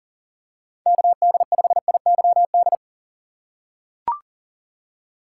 Morse code, heard as a keyed tone, about 40 wpm, 690 Hz. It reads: KD5IYD
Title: USA Call Signs - 6 Characters - 2x Character Spacing - 40wpm